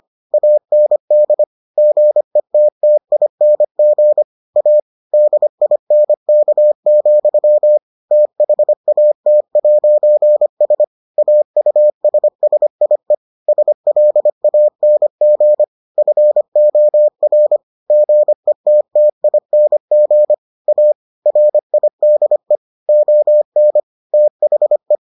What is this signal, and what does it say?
Morse code, heard as a keyed tone, about 25 wpm, 600 Hz. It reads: AND GETTING A DINK, THAT'S AUSSIE SLANG FOR GETTING A RIDE ON THE